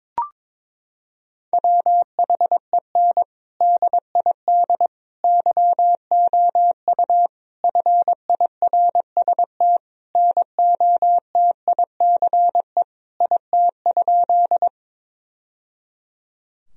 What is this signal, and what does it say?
Morse code, heard as a keyed tone, about 22 wpm, 700 Hz. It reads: WHEN DID YOU FIRST NOTICE IT?
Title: Sentences from Top 1000 Words - Review Part 2 - 22wpm